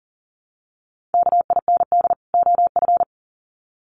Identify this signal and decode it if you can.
Morse code, heard as a keyed tone, about 40 wpm, 700 Hz. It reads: KIND OF
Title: Sets of 2 Words 40wpm